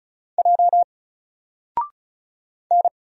Morse code, heard as a keyed tone, about 36 wpm, 700 Hz. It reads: J N
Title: Single Letter-Number Alt 35wpm